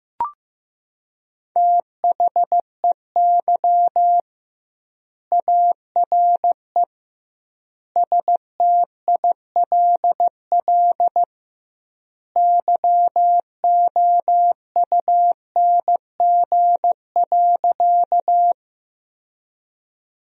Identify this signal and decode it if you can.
Morse code, heard as a keyed tone, about 15 wpm, 700 Hz. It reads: THEY ARE STILL YOUNG.
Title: 2x Word-Spacing - Sentences from Top 400 Words - 15wpm